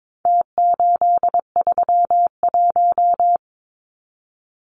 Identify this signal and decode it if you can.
Morse code, heard as a keyed tone, about 22 wpm, 700 Hz. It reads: T831
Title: Numbers - Contest Style - 4-Digit 22wpm